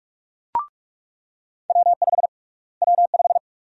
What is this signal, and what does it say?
Morse code, heard as a keyed tone, about 42 wpm, 700 Hz. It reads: W5 W5
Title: USA Call Sign Prefixes - Repeated 2x - 45wpm